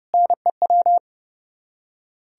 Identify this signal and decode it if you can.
Morse code, heard as a keyed tone, about 38 wpm, 700 Hz. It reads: NEW